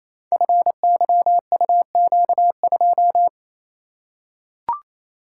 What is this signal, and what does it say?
Morse code, heard as a keyed tone, about 29 wpm, 700 Hz. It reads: FYUQ2
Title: Code Groups - 5 Letter:Numbers - 28wpm